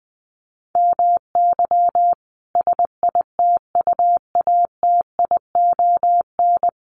Morse code, heard as a keyed tone, about 20 wpm, 700 Hz. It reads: MY SITUATION
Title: Sentences from Top 1200 Words 20wpm